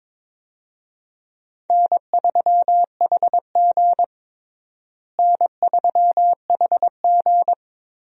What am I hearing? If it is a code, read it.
Morse code, heard as a keyed tone, about 22 wpm, 700 Hz. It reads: N3HG N3HG
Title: USA Call Signs - 4 Characters - Repeated 3x - 22wpm